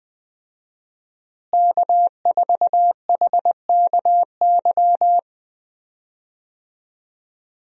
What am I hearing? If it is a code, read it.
Morse code, heard as a keyed tone, about 20 wpm, 700 Hz. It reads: K4HKY
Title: USA Call Signs - 5 Characters - Repeated 2x - 20wpm